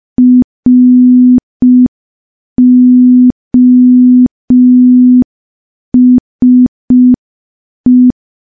Morse code, heard as a keyed tone, about 5 wpm, 250 Hz. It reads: ROSE
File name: morse / morse-training-05WPM